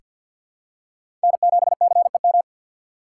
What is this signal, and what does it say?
Morse code, heard as a keyed tone, about 48 wpm, 700 Hz. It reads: N7CEM
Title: USA Call Signs - 5 Characters - 4x Character Spacing - 50wpm